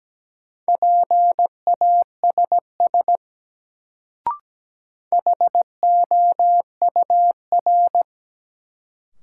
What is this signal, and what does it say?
Morse code, heard as a keyed tone, about 17 wpm, 700 Hz. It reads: PASS HOUR
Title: Sentences from Top 400 Words - New Word Drill - 17wpm